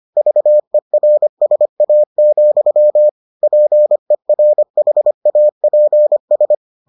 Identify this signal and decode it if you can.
Morse code, heard as a keyed tone, about 25 wpm, 600 Hz. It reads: VERSA, PERHAPS